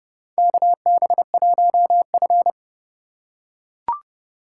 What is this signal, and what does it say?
Morse code, heard as a keyed tone, about 30 wpm, 700 Hz. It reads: KB1F
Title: USA Call Signs - 4 Characters - 3x Character Spacing - 30wpm